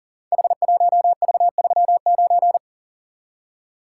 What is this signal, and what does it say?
Morse code, heard as a keyed tone, about 40 wpm, 700 Hz. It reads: H1V39
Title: Code Groups - 5 Letter:Numbers - 40wpm